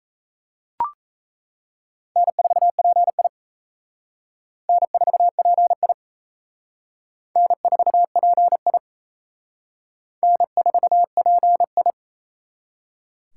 Morse code, heard as a keyed tone, about 35 wpm, 700 Hz. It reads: N4PI N4PI N4PI N4PI
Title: Speed Racing - USA 1x2 Call Signs 28wpm